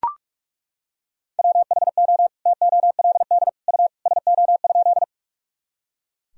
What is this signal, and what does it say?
Morse code, heard as a keyed tone, about 41 wpm, 700 Hz. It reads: WHO TOLD U SO?
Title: Sentences from Top 400 Words and Abbreviations 45wpm